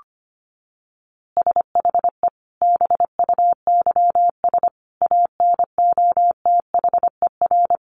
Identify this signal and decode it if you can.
Morse code, heard as a keyed tone, about 25 wpm, 700 Hz. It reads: SHE BUYS ANOTHER
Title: Easy Phrases 25wpm